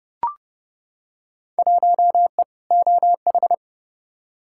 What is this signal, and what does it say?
Morse code, heard as a keyed tone, about 30 wpm, 700 Hz. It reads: 1E OH
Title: Field Day Exchanges - 30wpm